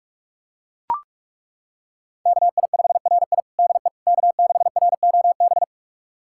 Morse code, heard as a keyed tone, about 41 wpm, 700 Hz. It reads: KI5RI DE K6ROB
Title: USA Callsign DE Callsign - 5 Characters - 45wpm